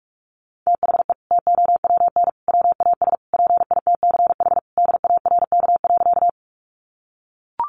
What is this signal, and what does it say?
Morse code, heard as a keyed tone, about 44 wpm, 700 Hz. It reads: THE TOWN WAS PITCH DARK.
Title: Sentences from Top 1300 Words 45wpm